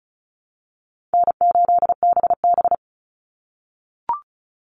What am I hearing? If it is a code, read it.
Morse code, heard as a keyed tone, about 34 wpm, 700 Hz. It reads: N8BB
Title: NA SOTA Chaser Callsigns 35wpm